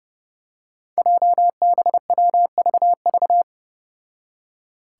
Morse code, heard as a keyed tone, about 28 wpm, 700 Hz. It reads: JBWVV
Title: Five Challenging Letter-Number Combinations 30wpm